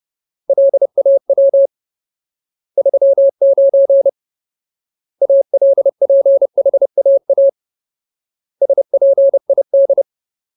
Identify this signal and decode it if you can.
Morse code, heard as a keyed tone, about 30 wpm, 550 Hz. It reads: LAW 39 ALPHAA SPID
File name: Ultimate Word List / 30WPM Ultimate